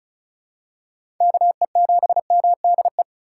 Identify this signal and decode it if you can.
Morse code, heard as a keyed tone, about 36 wpm, 700 Hz. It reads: KE7MDE
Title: USA Call Signs - 6 Characters - 4x Character Spacing - 35wpm